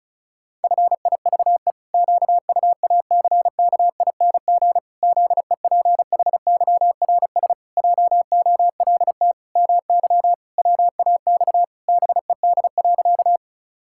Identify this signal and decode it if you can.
Morse code, heard as a keyed tone, about 36 wpm, 700 Hz. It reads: FIVE QUACKING ZEPHYRS JOLT MY WAX BED.
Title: Pangram Challenge v2 - 35wpm